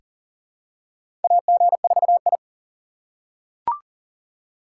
Morse code, heard as a keyed tone, about 40 wpm, 700 Hz. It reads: AG4I
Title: USA Call Signs - 2x Character Spacing - 40wpm